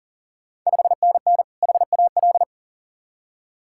Morse code, heard as a keyed tone, about 40 wpm, 700 Hz. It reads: 5NN HAL
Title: Ontario QSO Party Exchange 40wpm